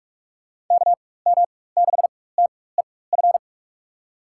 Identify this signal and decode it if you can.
Morse code, heard as a keyed tone, about 43 wpm, spaced as for 21 wpm, 700 Hz. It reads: KM6TEF
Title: USA Call Signs - 6 Characters - 4x Character Spacing - 45wpm